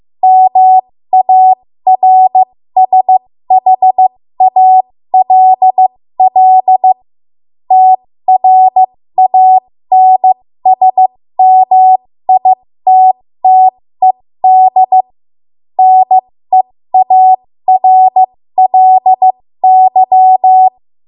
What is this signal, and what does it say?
Morse code, heard as a keyed tone, about 15 wpm, spaced as for 13 wpm, 750 Hz. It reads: MARSHALL TRANSMITTED NEARLY
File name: Morse / 13WPM